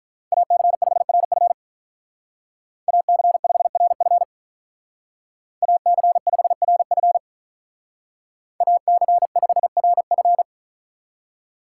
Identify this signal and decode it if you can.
Morse code, heard as a keyed tone, about 45 wpm, 700 Hz. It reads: AC5RF AC5RF AC5RF AC5RF